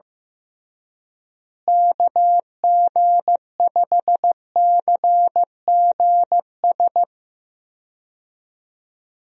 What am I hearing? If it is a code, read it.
Morse code, heard as a keyed tone, about 15 wpm, 700 Hz. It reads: KG5CGS
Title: USA Call Signs - 6 Characters - Repeated 3x - 15wpm